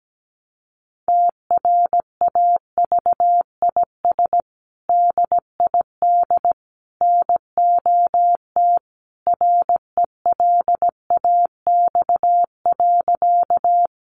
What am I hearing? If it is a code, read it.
Morse code, heard as a keyed tone, about 17 wpm, 700 Hz. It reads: TRAVIS DID NOT RELAX.
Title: A Sound Of Thunder 17wpm